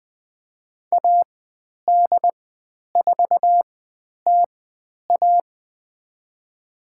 Morse code, heard as a keyed tone, about 20 wpm, spaced as for 10 wpm, 700 Hz. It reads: AD4TA
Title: 109 Lesson 18g - Letter m - Call Signs 20wpm